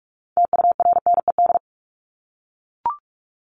Morse code, heard as a keyed tone, about 44 wpm, 700 Hz. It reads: TURNED
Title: ED Suffix - Top 100 Words with Base Words 45wpm